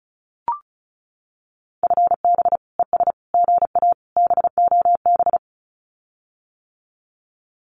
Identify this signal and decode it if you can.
Morse code, heard as a keyed tone, about 34 wpm, 700 Hz. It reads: FB ES GA BOB
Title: Common QSO Phrases Encore 35wpm